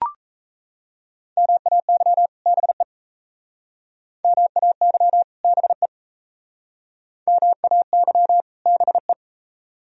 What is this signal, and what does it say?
Morse code, heard as a keyed tone, about 35 wpm, 690 Hz. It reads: MAY BE MAY BE MAY BE
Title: Speed-Racing - Sets of 2 Words 28wpm